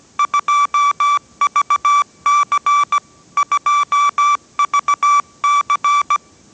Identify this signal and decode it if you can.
Morse code, heard as a keyed tone, about 17 wpm, 1160 Hz. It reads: 2VC 2VC
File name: 18wpm